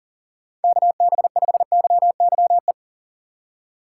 Morse code, heard as a keyed tone, about 48 wpm, 700 Hz. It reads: KB5YYE
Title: Speed Racing - USA Call Signs 40wpm